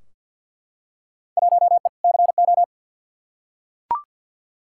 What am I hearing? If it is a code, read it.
Morse code, heard as a keyed tone, about 46 wpm, 700 Hz. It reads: 1E CO
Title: Field Day Exchanges - 50wpm